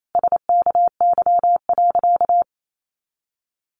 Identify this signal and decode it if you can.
Morse code, heard as a keyed tone, about 28 wpm, 700 Hz. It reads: SKY.
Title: A Sound Of Thunder 28wpm